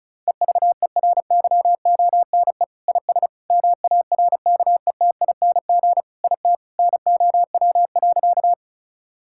Morse code, heard as a keyed tone, about 36 wpm, 690 Hz. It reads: EVERYONE IS MARKETING IT NOW.
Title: Sentences from Top 1000 Words - Review Part 5 - 35wpm